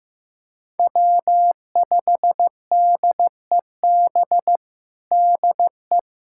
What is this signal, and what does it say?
Morse code, heard as a keyed tone, about 15 wpm, 700 Hz. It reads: W5DEB DE
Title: USA Callsign DE Callsign - 5 Characters - 15wpm